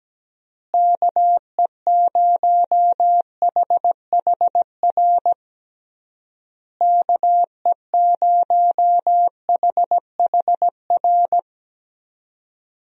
Morse code, heard as a keyed tone, about 17 wpm, 700 Hz. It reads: KE0HHR KE0HHR
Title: USA Call Signs - 6 Characters - Repeated 3x - 17wpm